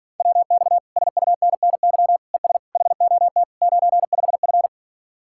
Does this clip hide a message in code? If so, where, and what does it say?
Yes: Morse code from 0.2 to 4.7 s, heard as a keyed tone, about 52 wpm, 700 Hz. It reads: WX SUNNY ES HOT 95F